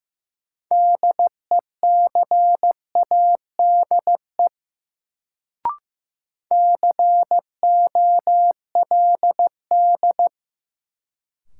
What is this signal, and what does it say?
Morse code, heard as a keyed tone, about 15 wpm, 700 Hz. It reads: DECADE COLD
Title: Sets of 5 Words - New Word Drill - 15wpm